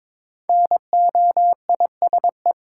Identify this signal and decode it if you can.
Morse code, heard as a keyed tone, about 23 wpm, 700 Hz. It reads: NOISE
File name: 5 Letter Words - First 3 Letters Given - 22wpm